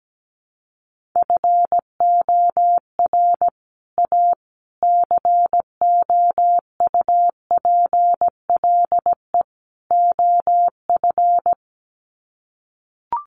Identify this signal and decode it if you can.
Morse code, heard as a keyed tone, about 17 wpm, 700 Hz. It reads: FOR A COUPLE OF
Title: Sets of 4 Words 17wpm